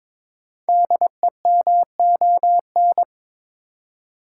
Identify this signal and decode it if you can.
Morse code, heard as a keyed tone, about 22 wpm, 690 Hz. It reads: DEMON
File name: 5 Letter Words - First 3 Letters Given - 22wpm